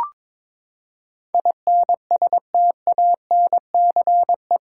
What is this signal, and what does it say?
Morse code, heard as a keyed tone, about 21 wpm, 700 Hz. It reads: INSTANCE
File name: practice / 22wpm / Eight Letter Words - 22wpm